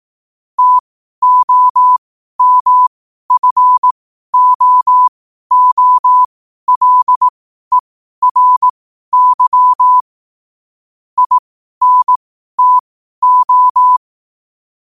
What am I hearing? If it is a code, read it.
Morse code, heard as a keyed tone, about 18 wpm, spaced as for 12 wpm, 1000 Hz. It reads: TOMFOOLERY INTO
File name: qotdfiles / 12wpm